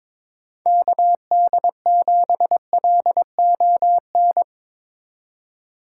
Morse code, heard as a keyed tone, about 22 wpm, 700 Hz. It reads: KD7LON
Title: USA Call Signs - 6 Characters - 4x Character Spacing - 22wpm